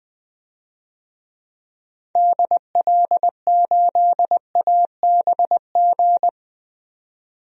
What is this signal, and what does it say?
Morse code, heard as a keyed tone, about 20 wpm, 700 Hz. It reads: DL8ABG
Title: DXCC-100 Call Signs - 6 Characters - 20wpm